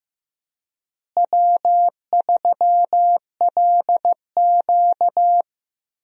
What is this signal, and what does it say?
Morse code, heard as a keyed tone, about 15 wpm, 700 Hz. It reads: W3LQ
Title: USA Call Signs - 4 Characters - Repeated 2x - 15wpm